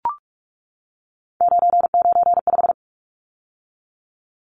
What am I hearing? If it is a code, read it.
Morse code, heard as a keyed tone, about 44 wpm, 700 Hz. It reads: 995
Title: Numbers - 3-Digit 45wpm